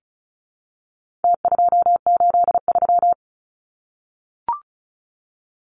Morse code, heard as a keyed tone, about 34 wpm, 700 Hz. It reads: T283
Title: Numbers - Contest Style - 4-Digit 35wpm